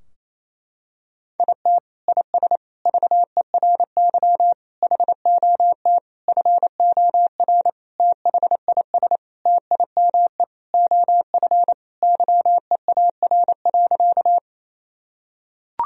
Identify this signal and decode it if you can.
Morse code, heard as a keyed tone, about 28 wpm, 700 Hz. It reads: IT IS VERY HOT FOR THIS TIME OF YEAR.
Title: Sentences from Top 1000 Words - Review Part 3 - 28wpm